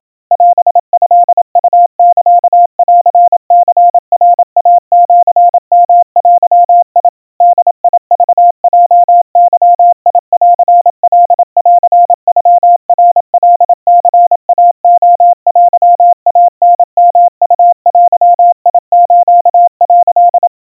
Morse code (Japanese Wabun code), heard as a keyed tone, about 27 wpm, 700 Hz. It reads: オトウサンニナイシヨデ ボクヲゲンカンノナカニイレテイタヨウデス」
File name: kawa / Morse sample / kamyu 100 cpm 700 Hz Crispy